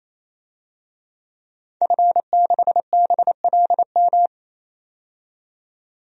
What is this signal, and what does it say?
Morse code, heard as a keyed tone, about 28 wpm, 700 Hz. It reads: F6BLM
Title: DXCC-100 Call Signs - 5 Characters - 28wpm